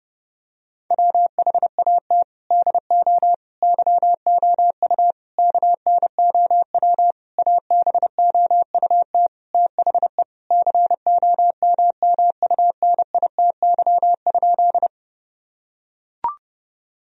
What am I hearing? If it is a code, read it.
Morse code, heard as a keyed tone, about 30 wpm, 700 Hz. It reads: WHAT DO YOU KNOW ABOUT THE COMMUNITY?
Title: Sentences from Top 1000 Words - Review Part 4 - 30wpm